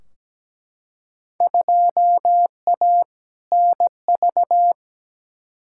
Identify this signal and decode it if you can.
Morse code, heard as a keyed tone, about 17 wpm, 700 Hz. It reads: 2A NV
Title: Field Day Exchanges - 17wpm